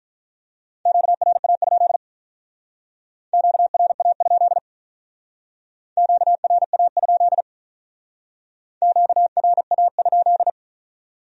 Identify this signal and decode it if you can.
Morse code, heard as a keyed tone, about 43 wpm, 700 Hz. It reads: QRA? QRA? QRA? QRA?
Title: Speed-Racing Q-Codes - 35wpm